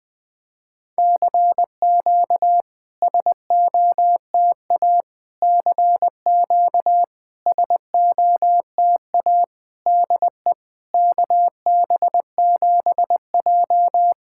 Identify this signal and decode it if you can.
Morse code, heard as a keyed tone, about 20 wpm, 700 Hz. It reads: CQ SOTA CQ SOTA DE KB7J
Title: SOTA QSO Practice 20wpm